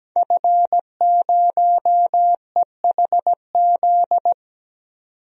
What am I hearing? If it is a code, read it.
Morse code, heard as a keyed tone, about 17 wpm, 700 Hz. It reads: F0EHZ
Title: DXCC-100 Call Signs - 5 Characters - 17wpm